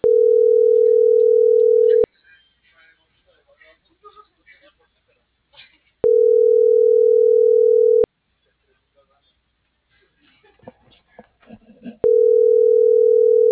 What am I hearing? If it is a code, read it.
Ringback tone (North America), 440 + 480 Hz.
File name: CENTRAL MAERSK / out adolfo.msk - .28025